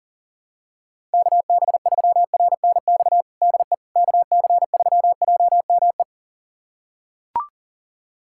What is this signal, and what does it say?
Morse code, heard as a keyed tone, about 40 wpm, 700 Hz. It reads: KB3RNX DE KC3JME
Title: USA Callsign DE Callsign 40wpm